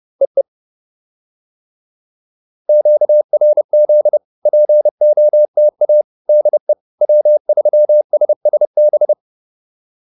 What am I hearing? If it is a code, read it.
Morse code, heard as a keyed tone, about 30 wpm, 600 Hz. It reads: EE QRZ POTA DE W3SSB